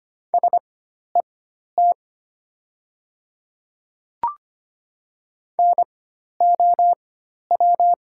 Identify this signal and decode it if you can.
Morse code, heard as a keyed tone, about 25 wpm, spaced as for 12 wpm, 700 Hz. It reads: SET NOW
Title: 4x Character Spacing - Top 200 Words - 25wpm